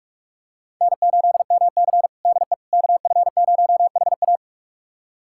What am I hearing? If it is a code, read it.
Morse code, heard as a keyed tone, about 41 wpm, 700 Hz. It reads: N8MC DE KF0HA
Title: USA Callsign DE Callsign 45wpm